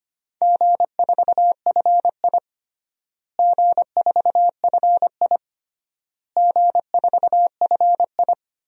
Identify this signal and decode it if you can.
Morse code, heard as a keyed tone, about 25 wpm, 700 Hz. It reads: G4FI G4FI G4FI
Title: DXCC-100 Call Signs - 4 Characters - Repeated 3x - 25wpm